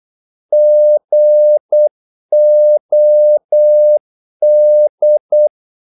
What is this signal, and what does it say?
Morse code, heard as a keyed tone, about 8 wpm, 600 Hz. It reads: GOD
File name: Korte ord DK 8wpm